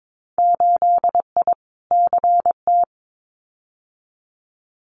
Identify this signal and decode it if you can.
Morse code, heard as a keyed tone, about 22 wpm, 700 Hz. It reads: 8I CT
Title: Winter Field Day Exchanges Encore 22wpm